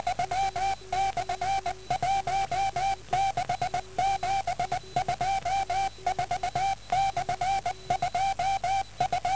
Morse code, heard as a keyed tone, about 12 wpm, 380 Hz, with noise. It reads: 16 MHZ K